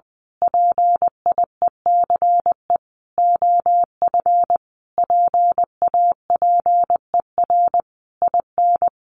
Morse code, heard as a keyed tone, about 20 wpm, 700 Hz. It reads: PIECE OF PAPER IN
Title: Sentences from Top 400 Words 20wpm